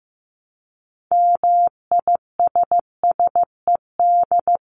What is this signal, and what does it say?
Morse code, heard as a keyed tone, about 15 wpm, 700 Hz. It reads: MISSED
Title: Common QSO Phrases 15wpm